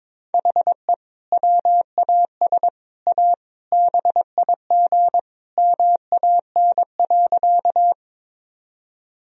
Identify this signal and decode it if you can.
Morse code, heard as a keyed tone, about 22 wpm, 700 Hz. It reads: HE WAS A BIG MAN.
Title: Sentences from Top 1000 Words - Review Part 1 - 22wpm